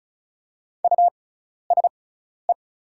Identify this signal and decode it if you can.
Morse code, heard as a keyed tone, about 36 wpm, spaced as for 12 wpm, 700 Hz. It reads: USE
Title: 6x Character Spacing - Top 300 Words - 35wpm